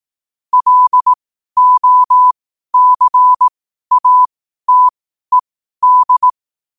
Morse code, heard as a keyed tone, about 18 wpm, spaced as for 12 wpm, 1000 Hz. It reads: LOCATED
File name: qotdfiles / 12wpm